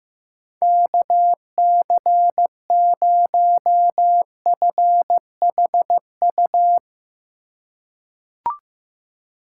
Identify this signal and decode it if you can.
Morse code, heard as a keyed tone, about 15 wpm, 700 Hz. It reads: KC0FHU
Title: USA Call Signs - 6 Characters - 2x Character Spacing - 15wpm